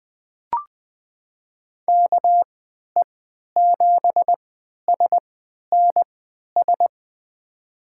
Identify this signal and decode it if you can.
Morse code, heard as a keyed tone, about 20 wpm, spaced as for 11 wpm, 700 Hz. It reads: KE7SNS
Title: USA Call Signs - 3x Character Spacing - 20wpm